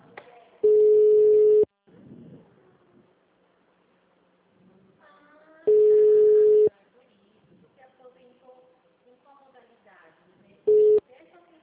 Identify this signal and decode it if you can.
Ringback tone (Europe, 425 Hz), with speech.